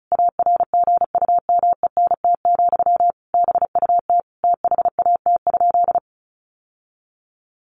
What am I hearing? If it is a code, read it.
Morse code, heard as a keyed tone, about 34 wpm, 700 Hz. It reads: ARGUMENT, BUT THAT?
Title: Sentences from Top 1400 Words 35wpm